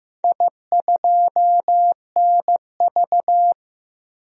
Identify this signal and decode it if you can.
Morse code, heard as a keyed tone, about 15 wpm, 700 Hz. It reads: I2NV
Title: DXCC-100 Call Signs - 4 Characters - 15wpm